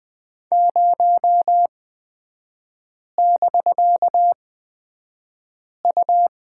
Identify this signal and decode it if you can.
Morse code, heard as a keyed tone, about 20 wpm, spaced as for 5 wpm, 700 Hz. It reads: 0<BK>U
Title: 250 Lesson 40e - BK - Three Characters 20wpm